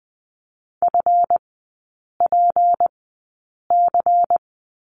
Morse code, heard as a keyed tone, about 20 wpm, spaced as for 8 wpm, 700 Hz. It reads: FPC
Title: 184 Lesson 30e - Letter b - Three Characters 20wpm